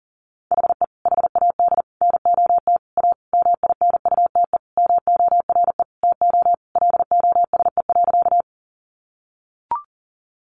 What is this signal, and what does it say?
Morse code, heard as a keyed tone, about 37 wpm, 700 Hz. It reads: HE HAD NOT A MINUTE MORE TO LOSE.
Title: Sentences from Top 1100 Words 40wpm